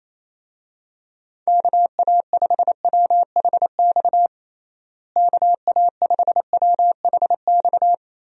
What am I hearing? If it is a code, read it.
Morse code, heard as a keyed tone, about 28 wpm, 700 Hz. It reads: KA5WHX KA5WHX
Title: USA Call Signs - 6 Characters - Repeated 3x - 28wpm